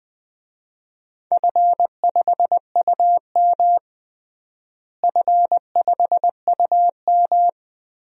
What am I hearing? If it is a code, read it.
Morse code, heard as a keyed tone, about 20 wpm, 700 Hz. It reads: F5UM F5UM
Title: DXCC-100 Call Signs - 4 Characters - Repeated 2x - 20wpm